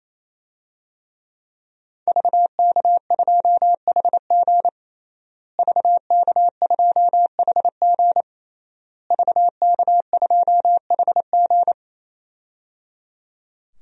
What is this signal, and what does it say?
Morse code, heard as a keyed tone, about 28 wpm, 700 Hz. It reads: VK2HG VK2HG VK2HG
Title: DXCC-100 Call Signs - 5 Characters - Repeated 3x - 28wpm